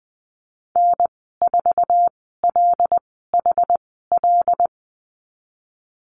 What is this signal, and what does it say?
Morse code, heard as a keyed tone, about 20 wpm, spaced as for 15 wpm, 700 Hz. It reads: N4LHL
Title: 076 Lesson 13g - Letter L - Call Signs 20wpm